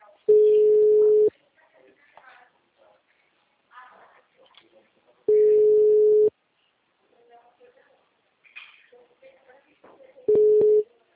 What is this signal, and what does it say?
Ringback tone (Europe, 425 Hz), with speech.
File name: out- -2009- - - .31712